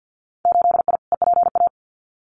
Morse code, heard as a keyed tone, about 50 wpm, 700 Hz. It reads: 8I EPA
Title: Winter Field Day Exchanges Encore 50wpm